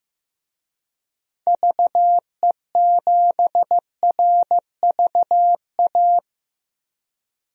Morse code, heard as a keyed tone, about 15 wpm, 700 Hz. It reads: VE7RVA
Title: DXCC-100 Call Signs - 6 Characters - 15wpm